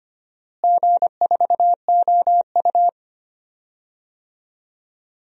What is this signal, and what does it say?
Morse code, heard as a keyed tone, about 25 wpm, 700 Hz. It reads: G4OU
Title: DXCC-100 Call Signs - 4 Characters - Repeated 3x - 25wpm